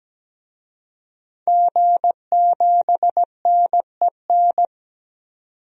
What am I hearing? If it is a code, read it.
Morse code, heard as a keyed tone, about 17 wpm, 700 Hz. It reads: G7NEN
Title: DXCC-100 Call Signs - 5 Characters - 17wpm